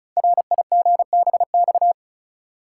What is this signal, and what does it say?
Morse code, heard as a keyed tone, about 36 wpm, 700 Hz. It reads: RIGBX
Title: Code Groups - 5 Letters - 35wpm